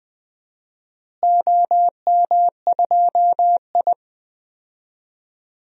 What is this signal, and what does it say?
Morse code, heard as a keyed tone, about 21 wpm, 700 Hz. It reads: OM2I
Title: DXCC-100 Call Signs - 4 Characters - Repeated 3x - 20wpm